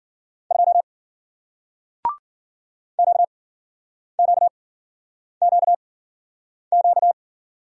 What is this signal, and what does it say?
Morse code, heard as a keyed tone, about 55 wpm, spaced as for 9 wpm, 700 Hz. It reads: 2 QQQQ